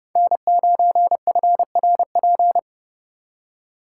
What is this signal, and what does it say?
Morse code, heard as a keyed tone, about 37 wpm, 700 Hz. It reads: N9FRP
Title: Speed Racing - USA 1x3 Call Signs 30wpm